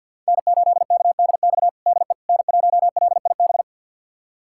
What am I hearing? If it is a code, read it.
Morse code, heard as a keyed tone, about 46 wpm, 690 Hz. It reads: N8KDX DE N1LIB
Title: USA Callsign DE Callsign - 5 Characters - 50wpm